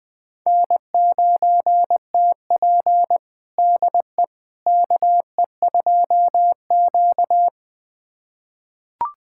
Morse code, heard as a keyed tone, about 20 wpm, 700 Hz. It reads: N9TP DE KE2Q
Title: USA Callsign DE Callsign - 4 Characters - 20wpm